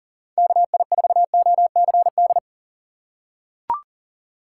Morse code, heard as a keyed tone, about 40 wpm, 700 Hz. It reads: KI4OCD
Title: USA Call Signs - 6 Characters - 4x Character Spacing - 40wpm